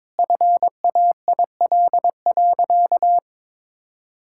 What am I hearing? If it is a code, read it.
Morse code, heard as a keyed tone, about 22 wpm, 700 Hz. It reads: FAIL.
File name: Sentences with 7-Word Phrases - 22wpm